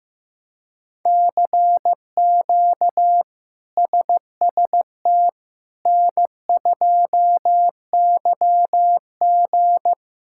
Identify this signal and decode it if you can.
Morse code, heard as a keyed tone, about 15 wpm, 700 Hz. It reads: CQ SST N2YG
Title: K1USN SST QSO Practice - 15wpm